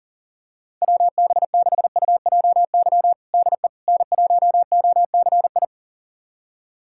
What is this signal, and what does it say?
Morse code, heard as a keyed tone, about 40 wpm, 700 Hz. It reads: WB6UJY DE N1OCI
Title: USA Callsign DE Callsign 40wpm